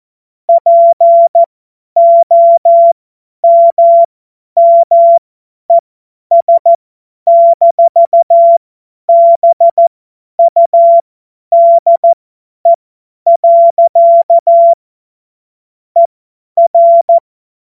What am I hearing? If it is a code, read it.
Morse code, heard as a keyed tone, about 14 wpm, spaced as for 10 wpm, 680 Hz. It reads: POMMES-BUDE. ER
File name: Morsekurs / Witze / Witz 13. Januar 2026 50 F70